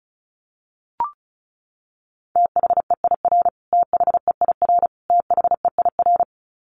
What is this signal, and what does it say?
Morse code, heard as a keyed tone, about 34 wpm, 700 Hz. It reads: THEIR THEIR THEIR
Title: Top 200 Words Repeated 35wpm